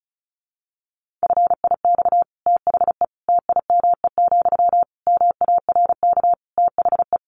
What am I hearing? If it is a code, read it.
Morse code, heard as a keyed tone, about 34 wpm, 700 Hz. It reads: FIX THE TIME, MARK THE
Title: Sentences from Top 1400 Words 35wpm